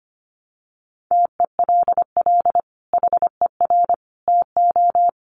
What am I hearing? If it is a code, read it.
Morse code, heard as a keyed tone, about 25 wpm, 700 Hz. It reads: TELL HER TO
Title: Sentences from Top 300 Words 25wpm